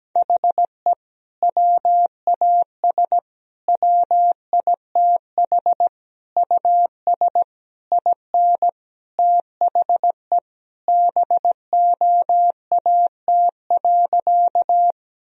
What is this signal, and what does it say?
Morse code, heard as a keyed tone, about 17 wpm, 700 Hz. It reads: HE WAS WITH US IN THE BOAT.
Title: Sentences from Top 1000 Words - Review Part 3 - 17wpm